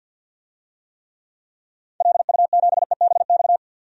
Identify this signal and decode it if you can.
Morse code, heard as a keyed tone, about 50 wpm, 700 Hz. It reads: LU7EBX
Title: DXCC-100 Call Signs - 6 Characters - 50wpm